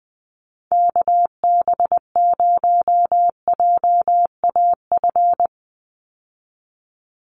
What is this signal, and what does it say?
Morse code, heard as a keyed tone, about 20 wpm, 700 Hz. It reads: KB0JAF
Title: USA Call Signs 20wpm